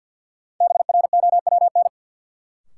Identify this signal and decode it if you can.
Morse code, heard as a keyed tone, about 48 wpm, 700 Hz. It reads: BROWN
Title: Sentences from Top 1200 Words - New Word Drill Alt - 50wpm